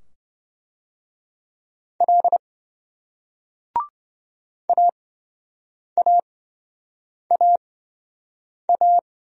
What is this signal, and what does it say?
Morse code, heard as a keyed tone, about 28 wpm, spaced as for 7 wpm, 700 Hz. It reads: L AAAA